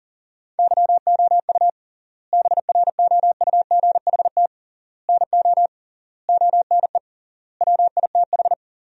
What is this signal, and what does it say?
Morse code, heard as a keyed tone, about 40 wpm, 700 Hz. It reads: YOU BROUGHT NO ONE WITH
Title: 3x Word-Spacing - Sentences from Top 500 Words - 40wpm